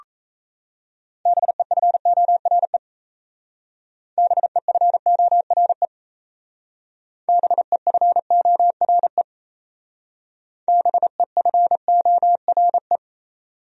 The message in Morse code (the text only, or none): BEFORE BEFORE BEFORE BEFORE